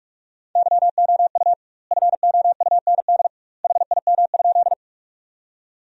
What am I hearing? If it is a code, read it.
Morse code, heard as a keyed tone, about 42 wpm, 700 Hz. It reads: YOU FOUND HIM?
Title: 2x Word-Spacing - Sentences from Top 300 Words - 45wpm